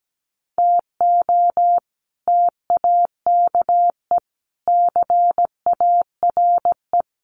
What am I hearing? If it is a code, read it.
Morse code, heard as a keyed tone, about 17 wpm, 700 Hz. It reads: TO TAKE CARE
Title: Sets of 4 Words 17wpm